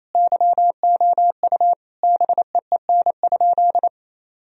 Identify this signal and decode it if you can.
Morse code, heard as a keyed tone, about 28 wpm, 700 Hz. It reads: YOU BEEN?
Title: Sentences from Top 1000 Words - Review Part 3 - 28wpm